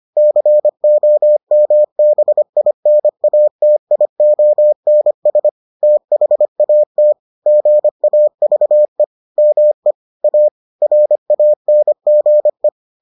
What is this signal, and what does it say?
Morse code, heard as a keyed tone, about 25 wpm, 600 Hz. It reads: COMBINATIONS THAT GAVE ME A RANGE